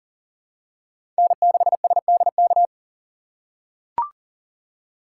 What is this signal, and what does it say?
Morse code, heard as a keyed tone, about 37 wpm, 700 Hz. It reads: N6SDK
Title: USA Call Signs - 5 Characters - 4x Character Spacing - 40wpm